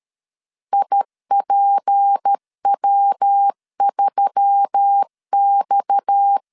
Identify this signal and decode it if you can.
Morse code, heard as a keyed tone, about 13 wpm, 800 Hz. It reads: IPW3X
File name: CW / DET / 1 6 1500C S A 12wpm 3440